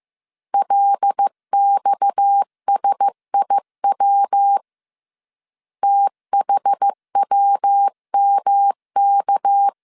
Morse code, heard as a keyed tone, about 15 wpm, 800 Hz. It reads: LXSIW THWMK
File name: CW / DET / 1 6 1500C S B 14wpm 2958